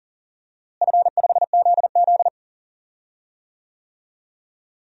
Morse code, heard as a keyed tone, about 40 wpm, 700 Hz. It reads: F5ZZ
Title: DXCC-100 Call Signs - 4 Characters - Repeated 3x - 40wpm